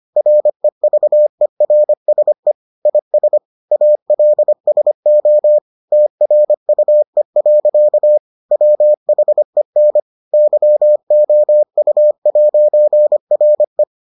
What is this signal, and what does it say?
Morse code, heard as a keyed tone, about 25 wpm, 600 Hz. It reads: REVERSE IS ALSO TRUE. WHEN YOU'RE